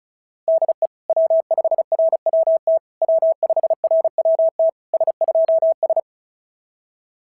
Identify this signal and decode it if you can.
Morse code, heard as a keyed tone, about 33 wpm, 660 Hz. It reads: DE W5RWT W5RWT S2S
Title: SOTA QSO Practice 35wpm